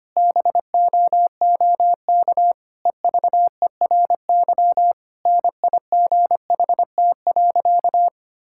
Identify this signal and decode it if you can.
Morse code, heard as a keyed tone, about 25 wpm, 700 Hz. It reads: BOOK EVERY NIGHT.
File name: practice / 25wpm / Difficult Sentences with Hints - 25wpm